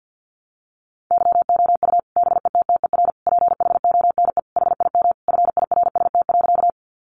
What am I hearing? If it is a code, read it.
Morse code, heard as a keyed tone, about 50 wpm, 700 Hz. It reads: YOU BETTER PHONE HIM FIRST.
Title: Sentences from Top 1100 Words 50wpm